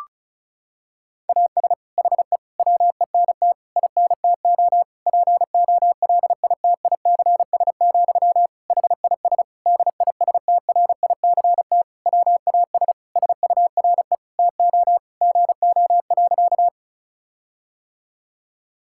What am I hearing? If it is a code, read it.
Morse code, heard as a keyed tone, about 36 wpm, 700 Hz. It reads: AS HE WENT INTO POLITICS, HIS DISTRICT WAS SURE TO GO.
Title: Sentences from Top 1600 Words 35wpm